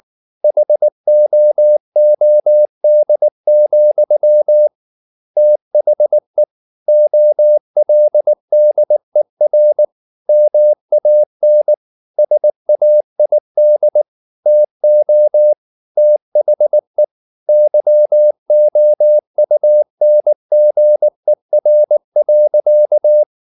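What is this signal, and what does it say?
Morse code, heard as a keyed tone, about 19 wpm, 600 Hz. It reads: HOOD, THE OLDER MAN SAID TO THE YOUNGER.
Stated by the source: Morse code at 19 wpm from CW med Gnister, reading Ken Follett's novel Never in English.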